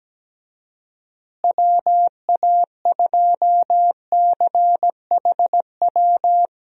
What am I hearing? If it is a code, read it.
Morse code, heard as a keyed tone, about 17 wpm, 700 Hz. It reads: WA2CHW
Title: USA Call Signs - 6 Characters - Repeated 3x - 17wpm